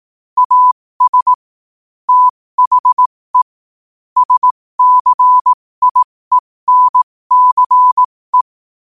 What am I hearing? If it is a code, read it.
Morse code, heard as a keyed tone, about 18 wpm, spaced as for 15 wpm, 1000 Hz. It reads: AS THE SCIENCE
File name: qotdfiles / 15wpm